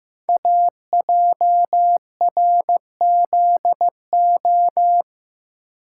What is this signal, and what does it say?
Morse code, heard as a keyed tone, about 15 wpm, 700 Hz. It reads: AJRZO
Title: Code Groups - 5 Letter:Numbers - 15wpm